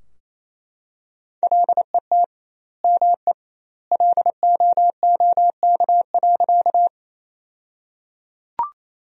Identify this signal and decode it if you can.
Morse code, heard as a keyed tone, about 28 wpm, 700 Hz. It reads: LET ME LOOK.
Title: 2x Word-Spacing - Sentences from Top 300 Words - 28wpm